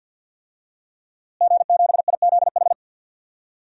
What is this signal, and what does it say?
Morse code, heard as a keyed tone, about 52 wpm, 700 Hz. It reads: G7IZH
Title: DXCC-100 Call Signs - 5 Characters - Repeated 2x - 50wpm